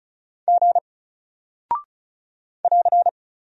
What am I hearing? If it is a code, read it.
Morse code, heard as a keyed tone, about 36 wpm, 700 Hz. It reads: G <AR>
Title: Single Character (Letter:Number:Punctuation:Prosign) Alt 35wpm